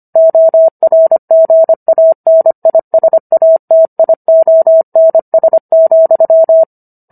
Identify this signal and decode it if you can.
Morse code, heard as a keyed tone, about 25 wpm, 650 Hz. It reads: ORGANISATIONS,